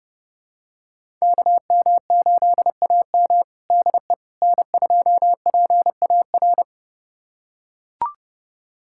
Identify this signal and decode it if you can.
Morse code, heard as a keyed tone, about 32 wpm, 700 Hz. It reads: KM8AM DE N2PAR
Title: USA Callsign DE Callsign - 5 Characters - 30wpm